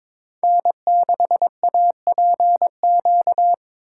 Morse code, heard as a keyed tone, about 26 wpm, 700 Hz. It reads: N6APQ
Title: Speed Racing - USA 1x3 Call Signs 22wpm